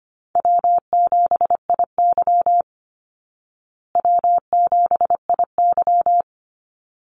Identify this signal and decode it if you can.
Morse code, heard as a keyed tone, about 25 wpm, 700 Hz. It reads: W7IY W7IY
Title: Easy Call Signs Repeated 2x - 25wpm